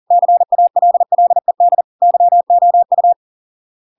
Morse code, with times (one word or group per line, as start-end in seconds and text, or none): 0.1-1.8 CALLED
2.0-3.1 YOU